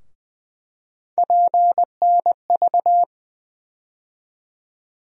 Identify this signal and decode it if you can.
Morse code, heard as a keyed tone, about 20 wpm, 700 Hz. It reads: PNV
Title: USA Call Sign Postfixes - 3 Letters 20wpm